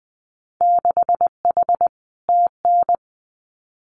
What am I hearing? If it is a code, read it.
Morse code, heard as a keyed tone, about 20 wpm, 700 Hz. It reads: 6H TN
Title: Winter Field Day Exchanges Encore 20wpm